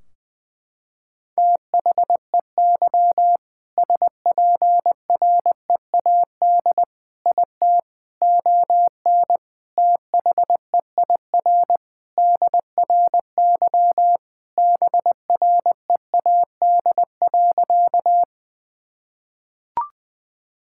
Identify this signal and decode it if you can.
Morse code, heard as a keyed tone, about 21 wpm, 700 Hz. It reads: THEY SPREAD IT ON THEIR DRY BREAD.
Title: Sentences from Top 1600 Words 20wpm